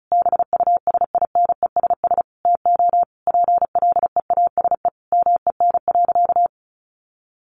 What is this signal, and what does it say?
Morse code, heard as a keyed tone, about 34 wpm, 700 Hz. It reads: BUSINESS TO PLEASE MEN.
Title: Sentences from Top 1500 Words 35wpm